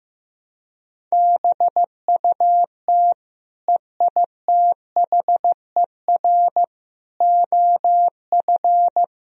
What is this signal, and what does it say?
Morse code, heard as a keyed tone, about 15 wpm, 700 Hz. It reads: BUT EITHER OF
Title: Sentences from Top 1000 Words - Review Part 4 - 15wpm